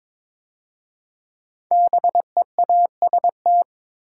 Morse code, heard as a keyed tone, about 23 wpm, 700 Hz. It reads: BEAST
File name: 5 Letter Words - First 3 Letters Given - 22wpm